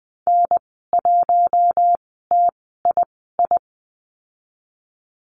Morse code, heard as a keyed tone, about 19 wpm, spaced as for 14 wpm, 700 Hz. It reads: N1TII
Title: 042 Lesson 8g - Number 1 - Call Signs 20wpm